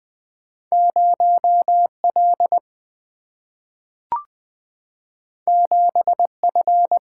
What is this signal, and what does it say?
Morse code, heard as a keyed tone, about 20 wpm, 700 Hz. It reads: 0L 7F
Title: Single-Letter Single-Number 20wpm